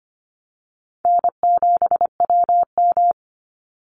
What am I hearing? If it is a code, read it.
Morse code, heard as a keyed tone, about 25 wpm, 700 Hz. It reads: N7WM
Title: NA SOTA Chaser Callsigns 25wpm